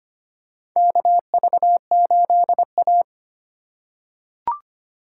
Morse code, heard as a keyed tone, about 25 wpm, 690 Hz. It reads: KV8A
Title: USA Call Signs - 4 Characters - 4x Character Spacing - 25wpm